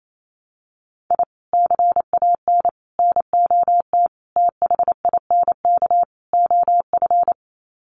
Morse code, heard as a keyed tone, about 28 wpm, 700 Hz. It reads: I CAN NOT THINK OF
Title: Sets of 5 Words 28wpm